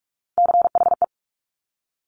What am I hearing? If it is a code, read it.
Morse code, heard as a keyed tone, about 46 wpm, 700 Hz. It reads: CHE
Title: Top Three Letter Combinations in Words 45wpm